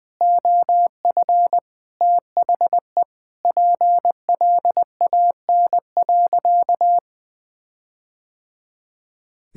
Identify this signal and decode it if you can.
Morse code, heard as a keyed tone, about 21 wpm, 700 Hz. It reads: OF THE PLAN.
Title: Sentences with 4-Word Phrases - Spoken Beginning - 20wpm